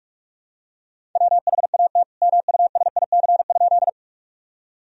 Morse code, heard as a keyed tone, about 46 wpm, 700 Hz. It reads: WHAT MUSIC?
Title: Sentences from Top 1000 Words - Review Part 1 - 45wpm